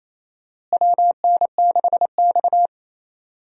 Morse code, heard as a keyed tone, about 28 wpm, 700 Hz. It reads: WN6X
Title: USA Call Signs - 4 Characters - Repeated 3x - 28wpm